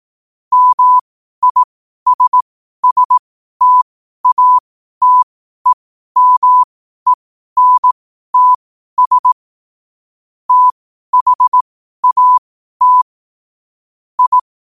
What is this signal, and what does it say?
Morse code, heard as a keyed tone, about 18 wpm, spaced as for 12 wpm, 1000 Hz. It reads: MISSTATEMENTS THAT I
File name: qotdfiles / 12wpm